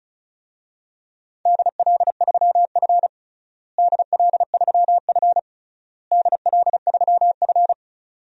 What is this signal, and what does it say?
Morse code, heard as a keyed tone, about 34 wpm, 700 Hz. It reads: DL3F DL3F DL3F
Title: DXCC-100 Call Signs - 4 Characters - Repeated 3x - 35wpm